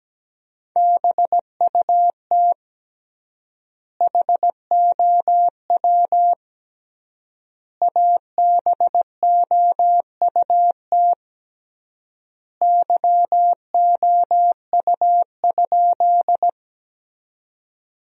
Morse code, heard as a keyed tone, about 17 wpm, 700 Hz. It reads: BUT HOW ABOUT YOU?
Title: 3x Word-Spacing - Sentences from Top 100 Words - 17wpm